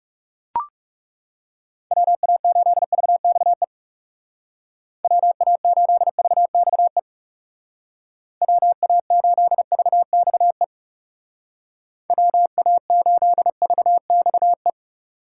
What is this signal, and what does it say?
Morse code, heard as a keyed tone, about 37 wpm, 700 Hz. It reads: WA8VXE WA8VXE WA8VXE WA8VXE
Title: Speed Racing - USA 2x3 Call Signs 30wpm